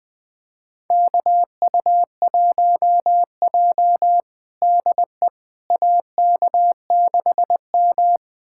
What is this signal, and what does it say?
Morse code, heard as a keyed tone, about 20 wpm, 700 Hz. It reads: KU1J DE AK6M
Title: USA Callsign DE Callsign - 4 Characters - 20wpm